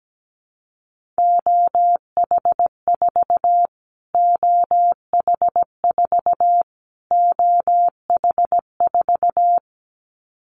Morse code, heard as a keyed tone, about 17 wpm, 700 Hz. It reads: OH4 OH4 OH4
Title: Callsign Prefixes for Top 25 DXCC - Repeated 3x - 17wpm